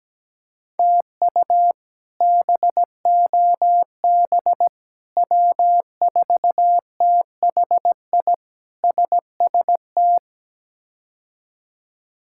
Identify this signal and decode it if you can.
Morse code, heard as a keyed tone, about 17 wpm, 700 Hz. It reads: TU BOB W4THI SST
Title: K1USN SST QSO Practice - 17wpm